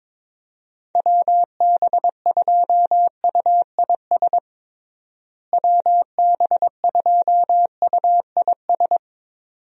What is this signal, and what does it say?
Morse code, heard as a keyed tone, about 22 wpm, 700 Hz. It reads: WB2UIS WB2UIS
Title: USA Call Signs - 6 Characters - Repeated 2x - 22wpm